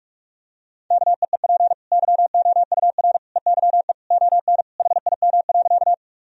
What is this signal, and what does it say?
Morse code, heard as a keyed tone, about 46 wpm, 700 Hz. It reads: KEEP YOUR EYE ON HIM.
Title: Sentences from Top 1000 Words - Review Part 1 - 45wpm